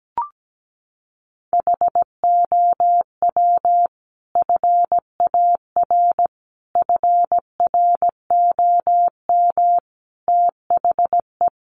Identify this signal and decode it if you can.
Morse code, heard as a keyed tone, about 17 wpm, 700 Hz. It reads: HOW FAR FROM THE
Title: Sentences from Top 400 Words 17wpm